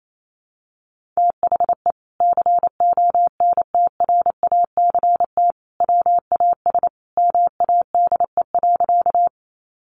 Morse code, heard as a keyed tone, about 28 wpm, 700 Hz. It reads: THE CONTRACT WAS MADE.
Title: Sentences from Top 1400 Words 28wpm